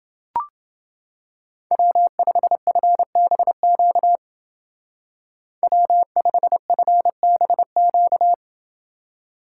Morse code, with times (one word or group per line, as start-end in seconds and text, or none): 1.7-4.2 W5FBQ
5.6-8.3 W5FBQ